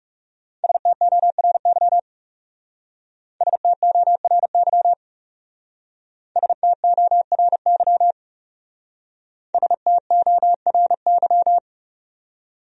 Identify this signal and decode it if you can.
Morse code, heard as a keyed tone, about 38 wpm, 700 Hz. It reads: STORY STORY STORY STORY